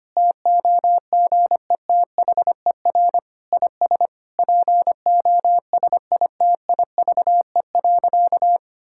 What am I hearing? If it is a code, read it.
Morse code, heard as a keyed tone, about 25 wpm, 700 Hz. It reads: TOGETHER IS POSITIVE.
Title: Sentences from Top 1000 Words - Review Part 5 - 25wpm